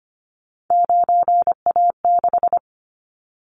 Morse code, heard as a keyed tone, about 25 wpm, 700 Hz. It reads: 9A6
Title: Callsign Prefixes for Top 50 DXCC - 25wpm